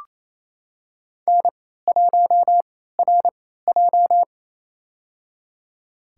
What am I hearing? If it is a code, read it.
Morse code, heard as a keyed tone, about 28 wpm, spaced as for 16 wpm, 700 Hz. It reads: N1RJ
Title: USA Call Signs - 4 Characters - 3x Character Spacing - 28wpm